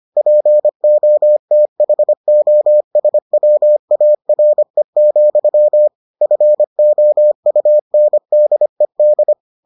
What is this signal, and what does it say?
Morse code, heard as a keyed tone, about 25 wpm, 600 Hz. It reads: POTHOSWARE, FOUNDED